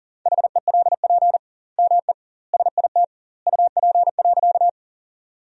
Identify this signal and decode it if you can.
Morse code, heard as a keyed tone, about 40 wpm, 700 Hz. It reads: HELP ME SIT UP.
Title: 2x Word-Spacing - Sentences from Top 500 Words - 40wpm